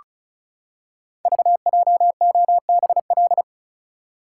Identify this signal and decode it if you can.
Morse code, heard as a keyed tone, about 36 wpm, 700 Hz. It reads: VJOBL
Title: Code Groups - 5 Letters - 35wpm